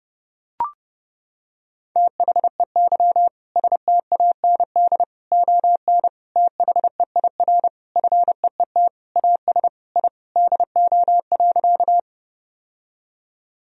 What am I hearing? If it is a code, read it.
Morse code, heard as a keyed tone, about 32 wpm, 700 Hz. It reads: THEY STAND ON THEIR FEET AS I DO.
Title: Sentences from Top 1000 Words - Review Part 1 - 30wpm